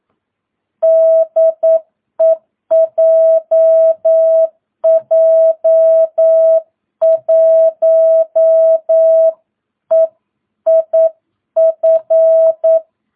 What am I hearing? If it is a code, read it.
Morse code, heard as a keyed tone, about 9 wpm, 650 Hz. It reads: DE JJ1EIF
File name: jh8cbh / 2022cw